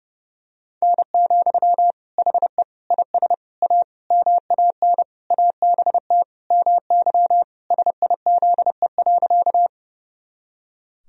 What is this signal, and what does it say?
Morse code, heard as a keyed tone, about 30 wpm, 700 Hz. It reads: N, HE IS A MAN ABT MY SIZE.
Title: Sentences from Top 400 Words and Abbreviations 30wpm